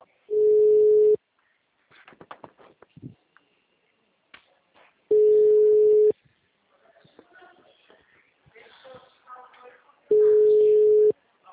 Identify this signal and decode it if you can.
Ringback tone (Europe, 425 Hz).